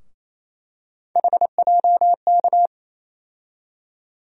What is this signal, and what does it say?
Morse code, heard as a keyed tone, about 28 wpm, 700 Hz. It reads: HJK
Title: USA Call Sign Postfixes - 3 Letters 28wpm